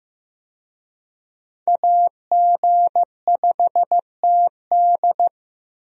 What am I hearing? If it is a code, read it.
Morse code, heard as a keyed tone, about 15 wpm, 700 Hz. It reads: AG5TD
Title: USA Call Signs - 5 Characters - Repeated 2x - 15wpm